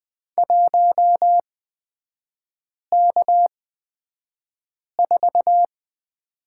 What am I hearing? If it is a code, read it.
Morse code, heard as a keyed tone, about 20 wpm, spaced as for 5 wpm, 700 Hz. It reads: 1K4
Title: 224 Lesson 36e - BT - Three Characters 20wpm